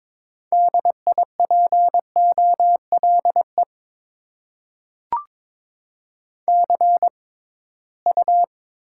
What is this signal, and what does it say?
Morse code, heard as a keyed tone, about 22 wpm, spaced as for 17 wpm, 700 Hz. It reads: DIPOLE CU
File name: Top 90 QSO Elements - 6x Character Spacing - 22wpm